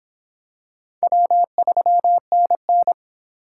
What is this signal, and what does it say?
Morse code, heard as a keyed tone, about 25 wpm, 700 Hz. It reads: W3NN